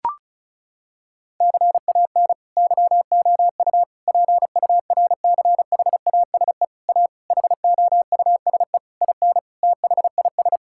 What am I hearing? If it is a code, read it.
Morse code, heard as a keyed tone, about 34 wpm, 700 Hz. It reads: CAN YOU PURCHASE A HOUSE IN THIS
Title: Sentences from Top 1000 Words - Review Part 5 - 35wpm